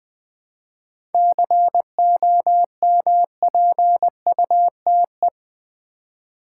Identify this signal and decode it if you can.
Morse code, heard as a keyed tone, about 21 wpm, 700 Hz. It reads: COMPUTE
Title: 7 Letter Words - First 3 Letters Given - 20wpm